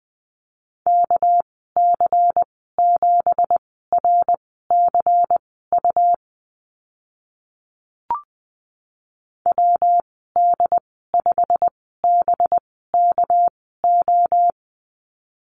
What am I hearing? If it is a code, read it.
Morse code, heard as a keyed tone, about 20 wpm, spaced as for 14 wpm, 700 Hz. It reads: KC7RCU WD5BKO
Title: 238 Lesson 38g - Letter q - Call Signs 20wpm